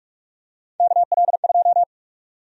Morse code, heard as a keyed tone, about 46 wpm, 700 Hz. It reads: KL2
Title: USA Call Sign Prefixes - 45wpm